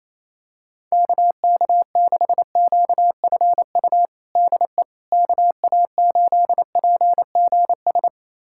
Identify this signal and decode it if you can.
Morse code, heard as a keyed tone, about 28 wpm, 700 Hz. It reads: KK6QFU DE KA8PGS
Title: USA Callsign DE Callsign - 6 Characters - 28wpm